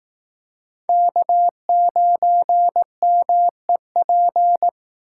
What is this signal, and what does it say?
Morse code, heard as a keyed tone, about 18 wpm, 700 Hz. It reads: K9MEP
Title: Speed Racing - USA 1x3 Call Signs 15wpm